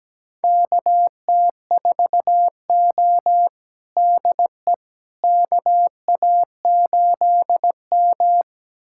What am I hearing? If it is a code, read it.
Morse code, heard as a keyed tone, about 17 wpm, 700 Hz. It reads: KT4O DE KA8M
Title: USA Callsign DE Callsign - 4 Characters - 17wpm